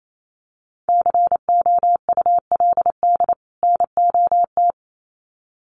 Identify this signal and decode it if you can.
Morse code, heard as a keyed tone, about 28 wpm, 700 Hz. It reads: COULD NOT
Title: Sets of 2 Words 28wpm